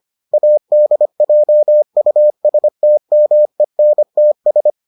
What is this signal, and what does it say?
Morse code, heard as a keyed tone, about 25 wpm, 600 Hz. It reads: ADJUSTMENTS